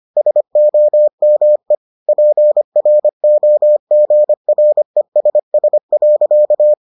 Morse code, heard as a keyed tone, about 25 wpm, 600 Hz. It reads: SOME PROGRESS.